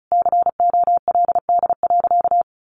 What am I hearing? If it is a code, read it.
Morse code, heard as a keyed tone, about 34 wpm, 700 Hz. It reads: COLD.
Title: Sentences from Top 400 Words 35wpm